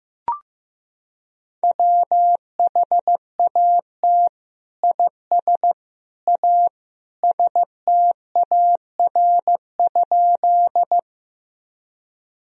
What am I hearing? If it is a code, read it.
Morse code, heard as a keyed tone, about 15 wpm, 700 Hz. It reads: WHAT IS A STAR?
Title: Sentences from Top 1000 Words - Review Part 2 - 15wpm